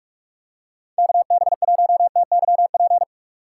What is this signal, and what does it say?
Morse code, heard as a keyed tone, about 49 wpm, 700 Hz. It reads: KB1TYP